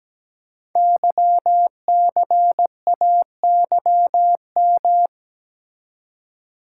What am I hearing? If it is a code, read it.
Morse code, heard as a keyed tone, about 17 wpm, 700 Hz. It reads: YCAYM
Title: Code Groups - 5 Letters - 17wpm